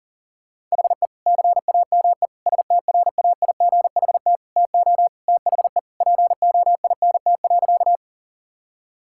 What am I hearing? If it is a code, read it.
Morse code, heard as a keyed tone, about 40 wpm, 690 Hz. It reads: HE CAME STRAIGHT TO THE POINT.
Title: Sentences from Top 1000 Words - Review Part 6 - 40wpm